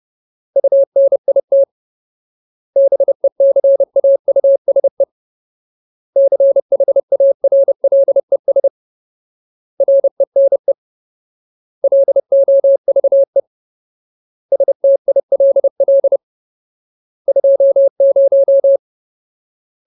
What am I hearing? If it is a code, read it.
Morse code, heard as a keyed tone, about 30 wpm, 550 Hz. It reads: UNIT BECAUSE CHARLES RENE LOVE STILL 20
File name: Ultimate Word List / 30WPM Ultimate